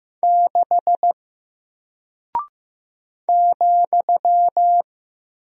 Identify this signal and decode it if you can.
Morse code, heard as a keyed tone, about 15 wpm, 700 Hz. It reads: 6 ,
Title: Single Character (Letter:Number:Punctuation:Prosign) 15wpm